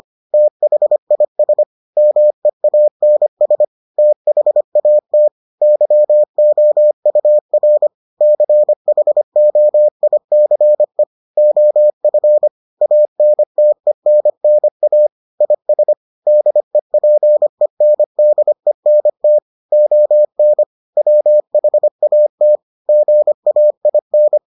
Morse code, heard as a keyed tone, about 25 wpm, 600 Hz. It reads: THIS MEANS THAT YOUR CHOICE OF ANTENNA IS DEPENDENT ON WHAT GAIN